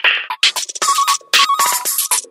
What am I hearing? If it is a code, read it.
There is no code.